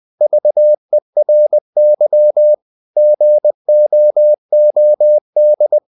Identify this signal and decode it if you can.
Morse code, heard as a keyed tone, about 20 wpm, 600 Hz. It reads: VERY GOOD
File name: the old man and the sea / the old man and the sea part 1 20wpm